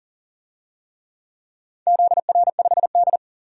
Morse code, heard as a keyed tone, about 40 wpm, 700 Hz. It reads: ZR5D
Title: DXCC-100 Call Signs - 4 Characters - 40wpm